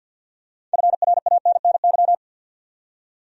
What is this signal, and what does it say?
Morse code, heard as a keyed tone, about 51 wpm, 700 Hz. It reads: FRANNY
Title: USA Common Names - 50wpm